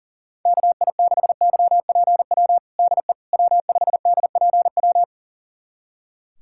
Morse code, heard as a keyed tone, about 40 wpm, 700 Hz. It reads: KI6YPW DE W5DPW
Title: USA Callsign DE Callsign 40wpm